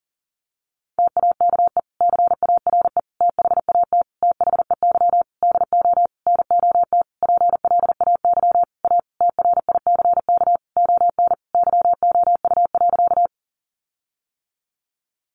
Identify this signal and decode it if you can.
Morse code, heard as a keyed tone, about 40 wpm, 700 Hz. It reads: TAKE CARE THAT THEY DO NOT PLAY A TRICK ON YOU.
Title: Sentences from Top 1500 Words 40wpm